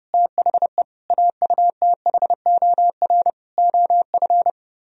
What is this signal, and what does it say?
Morse code, heard as a keyed tone, about 34 wpm, 700 Hz. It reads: THE AUTHOR OF
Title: Speed-Racing - Sets of 4 Words 30wpm